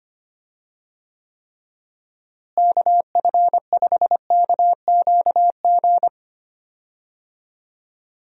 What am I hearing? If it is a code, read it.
Morse code, heard as a keyed tone, about 25 wpm, 700 Hz. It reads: KF5KQG
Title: USA Call Signs - 6 Characters - Repeated 2x - 25wpm